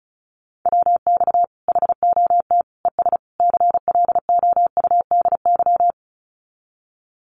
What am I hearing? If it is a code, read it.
Morse code, heard as a keyed tone, about 34 wpm, 700 Hz. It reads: WX HOT ES CLOUDY
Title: Common QSO Phrases 35wpm